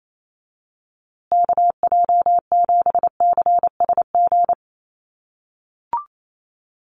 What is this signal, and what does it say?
Morse code, heard as a keyed tone, about 28 wpm, 700 Hz. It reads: KJ7CSG
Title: USA Call Signs Repeated 2x 28wpm